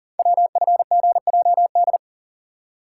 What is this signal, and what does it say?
Morse code, heard as a keyed tone, about 39 wpm, 700 Hz. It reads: WFGJD
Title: Code Groups - 5 Letter:Numbers - 40wpm